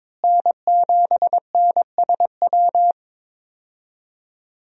Morse code, heard as a keyed tone, about 22 wpm, 700 Hz. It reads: N7NSW
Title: USA Call Signs - 5 Characters - Repeated 2x - 22wpm